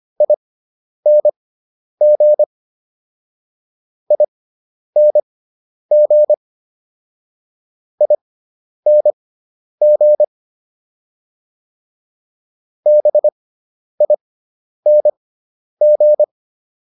Morse code, heard as a keyed tone, about 25 wpm, spaced as for 10 wpm, 600 Hz. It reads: ING ING ING BING